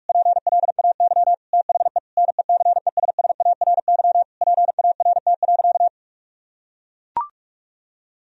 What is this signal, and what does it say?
Morse code, heard as a keyed tone, about 41 wpm, 700 Hz. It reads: PLAY THE NECESSARY PART.
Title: Sentences from Top 1000 Words - Review Part 1 - 45wpm